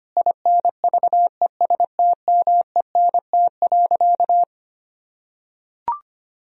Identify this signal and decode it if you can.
Morse code, heard as a keyed tone, about 25 wpm, 700 Hz. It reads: INVESTMENT.
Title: Sentences from Top 1000 Words - Review Part 4 - 25wpm